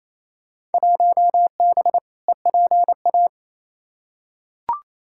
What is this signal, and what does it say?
Morse code, heard as a keyed tone, about 28 wpm, 700 Hz. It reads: 1B EPA
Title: Field Day Exchanges Encore - 28wpm